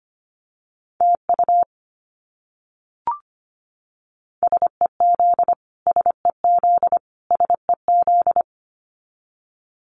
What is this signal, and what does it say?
Morse code, heard as a keyed tone, about 25 wpm, 700 Hz. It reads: TU SEZ SEZ SEZ
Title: Abbreviations Repeated 3x - 25wpm